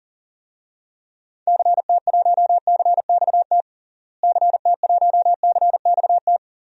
Morse code, heard as a keyed tone, about 37 wpm, 700 Hz. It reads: CT1CXT CT1CXT
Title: DXCC-100 Call Signs - 6 Characters - Repeated 3x - 40wpm